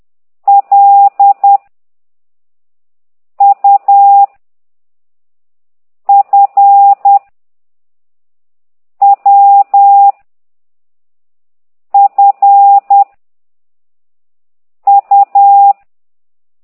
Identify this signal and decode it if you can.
Morse code, heard as a keyed tone, about 10 wpm, 800 Hz. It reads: L U F W F U
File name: ct2idw / exerc8